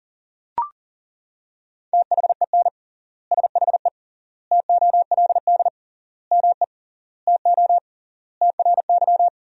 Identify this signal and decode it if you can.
Morse code, heard as a keyed tone, about 37 wpm, 700 Hz. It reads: THEN SHE TOLD ME TO TRY
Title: 3x Word-Spacing - Sentences from Top 400 Words - 40wpm